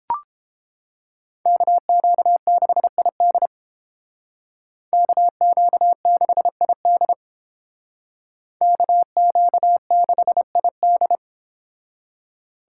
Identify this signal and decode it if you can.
Morse code, heard as a keyed tone, about 30 wpm, 700 Hz. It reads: KQ6ID KQ6ID KQ6ID